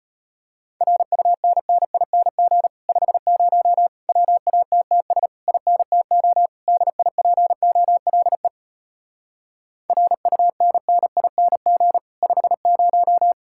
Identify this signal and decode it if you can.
Morse code, heard as a keyed tone, about 33 wpm, 700 Hz. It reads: RUNNING 50 WATTS INTO DIPOLE RUNNING 50